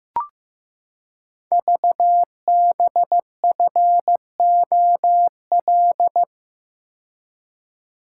Morse code, heard as a keyed tone, about 15 wpm, 700 Hz. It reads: VBFOL
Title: Code Groups - 5 Letters - 15wpm